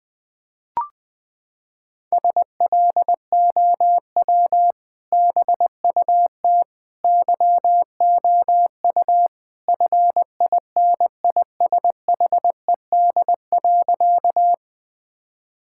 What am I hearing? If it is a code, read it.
Morse code, heard as a keyed tone, about 20 wpm, 700 Hz. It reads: SLOW BUT YOU FINISHED.
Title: Sentences with 2-Word Phrases - 20wpm